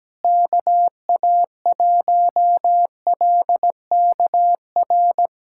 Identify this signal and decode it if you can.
Morse code, heard as a keyed tone, about 17 wpm, 700 Hz. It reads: KA1LKR
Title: USA Callsign DE Callsign - 6 Characters - 17wpm